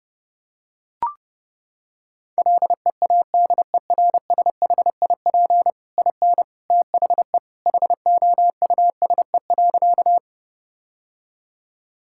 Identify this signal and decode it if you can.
Morse code, heard as a keyed tone, about 32 wpm, 700 Hz. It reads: LEADERSHIP IN THE HOUSE.
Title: Sentences from Top 1600 Words 30wpm